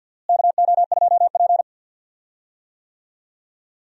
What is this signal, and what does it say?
Morse code, heard as a keyed tone, about 49 wpm, 700 Hz. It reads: KO2P
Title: USA Call Signs - 4 Characters - Repeated 2x - 50wpm